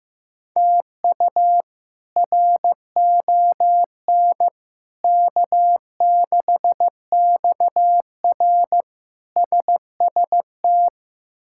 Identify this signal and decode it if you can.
Morse code, heard as a keyed tone, about 15 wpm, 700 Hz. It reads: TU RON K6XR SST
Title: K1USN SST QSO Practice - 15wpm